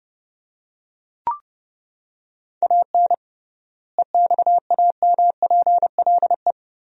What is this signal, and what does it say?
Morse code, heard as a keyed tone, about 30 wpm, 700 Hz. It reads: AN EXAMPLE
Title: 3x Word-Spacing - Sentences from Top 400 Words - 30wpm